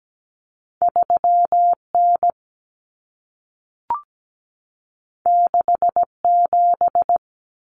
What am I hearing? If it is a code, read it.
Morse code, heard as a keyed tone, about 17 wpm, 700 Hz. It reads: 3N 67
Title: Common Cut Numbers - 2-Digit 17wpm